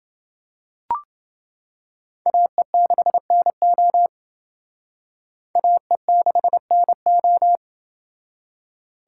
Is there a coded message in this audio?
Yes: Morse code from 2.3 to 7.6 s, heard as a keyed tone, about 29 wpm, 700 Hz. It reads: AE6NO AE6NO